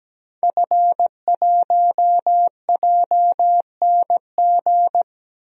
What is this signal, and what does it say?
Morse code, heard as a keyed tone, about 17 wpm, 700 Hz. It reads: F1JNG
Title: DXCC-100 Call Signs - 5 Characters - 17wpm